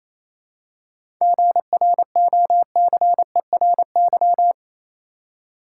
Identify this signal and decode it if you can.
Morse code, heard as a keyed tone, about 28 wpm, 700 Hz. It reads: GROCERY
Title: Seven Letter Words - 28wpm